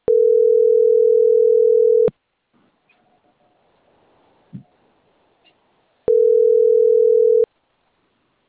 Ringback tone (North America), 440 + 480 Hz.